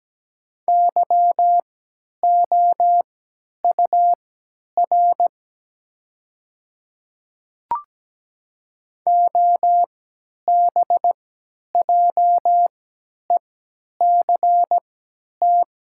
Morse code, heard as a keyed tone, about 17 wpm, spaced as for 10 wpm, 700 Hz. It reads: YOUR OBJECT
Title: 3x Character Spacing - Top 500 Words - 17wpm